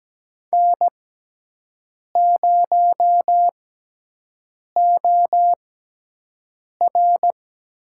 Morse code, heard as a keyed tone, about 17 wpm, spaced as for 6 wpm, 700 Hz. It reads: N0OR
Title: USA Call Signs - 6x Character Spacing - 17wpm